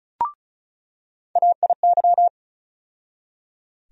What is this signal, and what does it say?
Morse code, heard as a keyed tone, about 35 wpm, 700 Hz. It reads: AIY
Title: USA Call Sign Postfixes - 3 Letters 35wpm